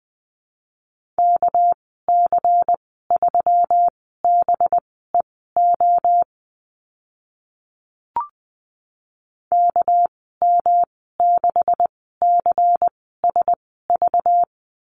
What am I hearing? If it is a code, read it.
Morse code, heard as a keyed tone, about 20 wpm, spaced as for 14 wpm, 700 Hz. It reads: KC3BEO KM6CSV
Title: 213 Lesson 34g - Number 8 - Call Signs 20wpm